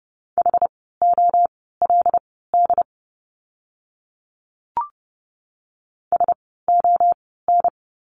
Morse code, heard as a keyed tone, about 30 wpm, spaced as for 17 wpm, 700 Hz. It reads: HOLD SON
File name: Lesson 17f - Letter u - Words 30wpm